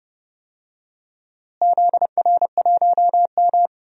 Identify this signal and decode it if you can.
Morse code, heard as a keyed tone, about 30 wpm, 700 Hz. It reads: ZR1M
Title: DXCC-100 Call Signs - 4 Characters - Repeated 3x - 30wpm